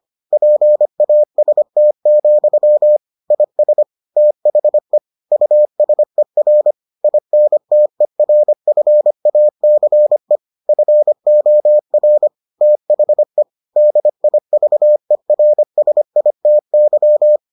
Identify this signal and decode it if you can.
Morse code, heard as a keyed tone, about 25 wpm, 600 Hz. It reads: PAST, IS THE USER INTERFACE FOR THE DIVERSITY